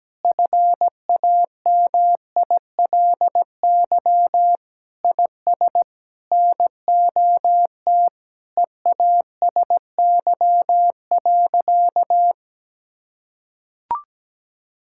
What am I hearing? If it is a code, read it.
Morse code, heard as a keyed tone, about 17 wpm, 700 Hz. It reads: FAMILY IS NOT EASY.
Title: Sentences from Top 1000 Words - Review Part 6 - 17wpm